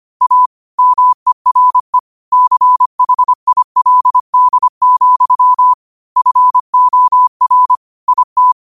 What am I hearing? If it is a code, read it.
Morse code, heard as a keyed tone, about 25 wpm, 1000 Hz. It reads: A MERE CHILD, FOR IT
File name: qotdfiles / 25wpm